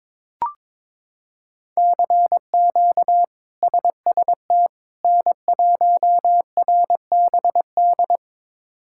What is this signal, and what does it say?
Morse code, heard as a keyed tone, about 22 wpm, 700 Hz. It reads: CQ SST N1RBD
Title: K1USN SST QSO Practice - 22wpm